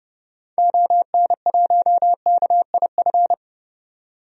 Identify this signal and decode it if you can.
Morse code, heard as a keyed tone, about 30 wpm, 700 Hz. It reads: ON1KIF
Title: DXCC-100 Call Signs - 6 Characters - 30wpm